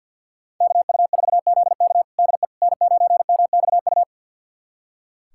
Morse code, heard as a keyed tone, about 46 wpm, 700 Hz. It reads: KU4ZK DE N9MXU
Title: USA Callsign DE Callsign - 5 Characters - 50wpm